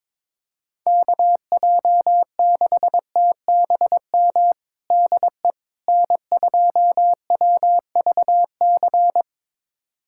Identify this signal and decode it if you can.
Morse code, heard as a keyed tone, about 21 wpm, 700 Hz. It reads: KJ6TBM DE N2WVC
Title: USA Callsign DE Callsign 22wpm